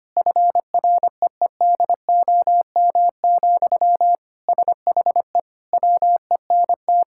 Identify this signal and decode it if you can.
Morse code, heard as a keyed tone, about 25 wpm, 700 Hz. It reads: FREEDOM, SHE WENT
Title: Sentences from Top 1000 Words - Review Part 5 - 25wpm